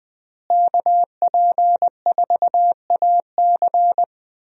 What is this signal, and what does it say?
Morse code, heard as a keyed tone, about 20 wpm, 700 Hz. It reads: KP4AC
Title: USA Call Signs - 5 Characters - Repeated 3x - 20wpm